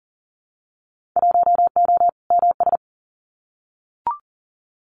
Morse code, heard as a keyed tone, about 40 wpm, 700 Hz. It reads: 1O MS
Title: Winter Field Day Exchanges 40wpm